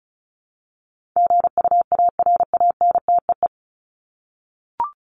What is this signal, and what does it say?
Morse code, heard as a keyed tone, about 34 wpm, 700 Hz. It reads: GUARANTEE
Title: Top 1501-1750 Words 35wpm